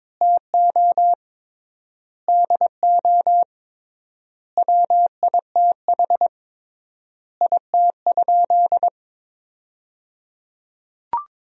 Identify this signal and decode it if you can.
Morse code, heard as a keyed tone, about 22 wpm, 700 Hz. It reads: TO DO WITH IT?
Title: 3x Word-Spacing - Sentences from Top 100 Words - 22wpm